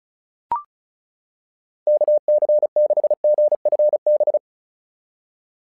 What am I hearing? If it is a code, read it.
Morse code, heard as a keyed tone, about 34 wpm, 600 Hz. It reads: KC6GFB
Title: SOTA QSO Practice 35wpm